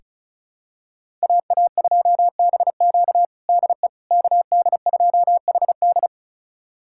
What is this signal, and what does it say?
Morse code, heard as a keyed tone, about 36 wpm, 700 Hz. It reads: AA2BQ DE KD2HD
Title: USA Callsign DE Callsign - 5 Characters - 35wpm